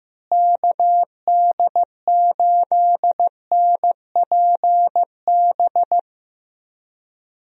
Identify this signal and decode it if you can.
Morse code, heard as a keyed tone, about 15 wpm, 700 Hz. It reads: KD8NPB
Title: USA Call Signs - 6 Characters - Repeated 3x - 15wpm